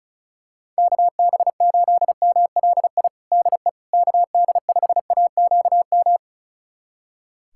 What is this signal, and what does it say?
Morse code, heard as a keyed tone, about 36 wpm, 690 Hz. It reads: KB8MLI DE KD5AQM
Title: USA Callsign DE Callsign 35wpm